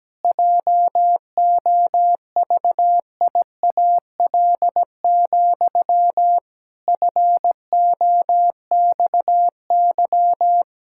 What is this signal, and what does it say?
Morse code, heard as a keyed tone, about 17 wpm, 700 Hz. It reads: JOVIAL, FOXY
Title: Pangram Challenge v2 - 17wpm